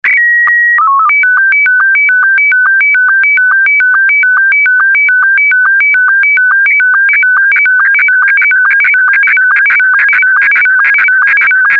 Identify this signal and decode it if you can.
SSTV (slow-scan television), Scottie 1.